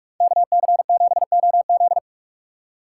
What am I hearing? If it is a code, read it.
Morse code, heard as a keyed tone, about 59 wpm, 700 Hz. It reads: KC7OZ